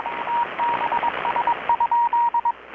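Morse code, heard as a keyed tone, about 21 wpm, 940 Hz, with radio noise. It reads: TBS?
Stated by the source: Morse from KPH's Night of Nights XX maritime radiotelegraphy event, received in Chiapas, Mexico about 3,800 km away on a Kenwood R-600 with a grounded 10 m long-wire antenna.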